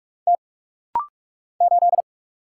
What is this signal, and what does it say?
Morse code, heard as a keyed tone, about 44 wpm, 700 Hz. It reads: T 8
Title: Single Letter-Number Alt - Rapid-Fire 45wpm